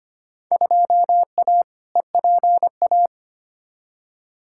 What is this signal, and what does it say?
Morse code, heard as a keyed tone, about 25 wpm, 700 Hz. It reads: 2A EPA
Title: Field Day Exchanges - 25wpm